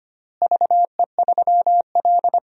Morse code, heard as a keyed tone, about 25 wpm, 700 Hz. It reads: VE3L
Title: DXCC-100 Call Signs - 4 Characters - Repeated 3x - 25wpm